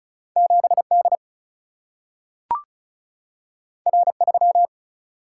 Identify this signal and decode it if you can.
Morse code, heard as a keyed tone, about 36 wpm, 700 Hz. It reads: 7 D R 3
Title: Single-Letter Single-Number 35wpm